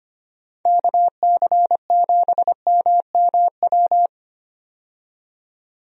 Morse code, heard as a keyed tone, about 25 wpm, 700 Hz. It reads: KC7MMW
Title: USA Call Signs - 6 Characters - 25wpm